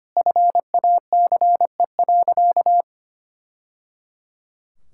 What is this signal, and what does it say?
Morse code, heard as a keyed tone, about 25 wpm, 700 Hz. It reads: FACE.
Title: Sentences from Top 1000 Words - Review Part 5 - 25wpm